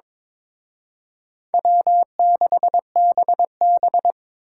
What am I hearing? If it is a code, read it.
Morse code, heard as a keyed tone, about 26 wpm, 700 Hz. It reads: W6BB
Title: Speed Racing - USA 1x2 Call Signs 22wpm